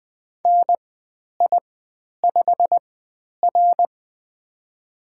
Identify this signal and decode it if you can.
Morse code, heard as a keyed tone, about 20 wpm, spaced as for 10 wpm, 700 Hz. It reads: NI5R
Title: 095 Lesson 16g - Letter c - Call Signs 20wpm